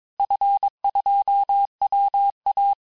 Morse code, heard as a keyed tone, about 22 wpm, 800 Hz. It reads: F2WA
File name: cours-du-mercredi / audios / cours-854-liste-fin-QSO-vitesse-21-mpm